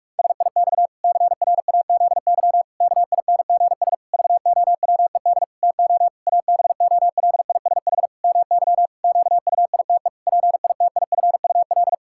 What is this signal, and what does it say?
Morse code, heard as a keyed tone, about 41 wpm, 700 Hz. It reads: SIX CRAZY KINGS VOWED TO ABOLISH MY QUITE PITIFUL
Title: Pangram Challenge v2 - 45wpm